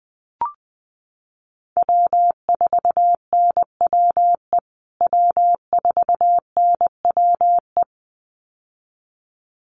Morse code, heard as a keyed tone, about 20 wpm, 700 Hz. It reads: W4NWE W4NWE
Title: USA Call Signs Repeated 2x 20wpm